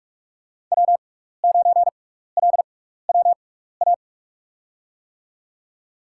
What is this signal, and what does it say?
Morse code, heard as a keyed tone, about 44 wpm, spaced as for 15 wpm, 700 Hz. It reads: W9LWA
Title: USA Call Signs - 5 Characters - 6x Character Spacing - 45wpm